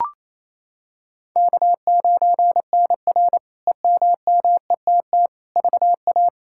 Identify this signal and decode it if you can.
Morse code, heard as a keyed tone, about 28 wpm, 700 Hz. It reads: K9NR EMMETT VA
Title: K1USN SST QSO Practice - 28wpm